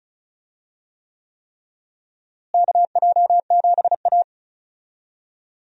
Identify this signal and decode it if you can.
Morse code, heard as a keyed tone, about 34 wpm, 700 Hz. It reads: KJ7A
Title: USA Call Signs - 4 Characters - Repeated 3x - 35wpm